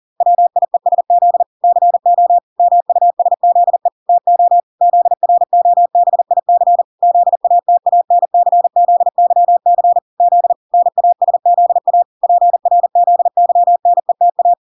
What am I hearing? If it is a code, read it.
Morse code, heard as a keyed tone, about 40 wpm, 700 Hz. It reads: WIESZ CO MUSZE TO ZROBIC ZATANCZYC Z NASZA PRZYNETA